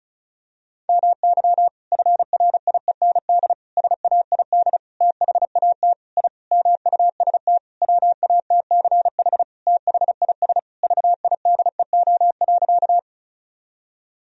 Morse code, heard as a keyed tone, about 36 wpm, 700 Hz. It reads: MY FRIEND SAID THAT I MUST WATCH THIS VIDEO.
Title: Sentences from Top 1000 Words - Review Part 4 - 35wpm